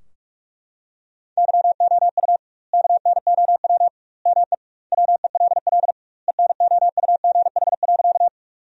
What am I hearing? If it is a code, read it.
Morse code, heard as a keyed tone, about 43 wpm, 700 Hz. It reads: YOU KNOW ME WELL ENOUGH.
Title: 2x Word-Spacing - Sentences from Top 400 Words - 45wpm